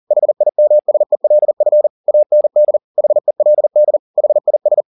Morse code, heard as a keyed tone, about 40 wpm, 600 Hz. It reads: HIMSELF AND HELD HIS